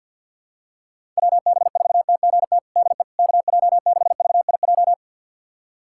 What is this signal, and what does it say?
Morse code, heard as a keyed tone, about 49 wpm, 700 Hz. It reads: WB4TGT DE KJ6VIJ
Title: USA Callsign DE Callsign - 6 Characters - 50wpm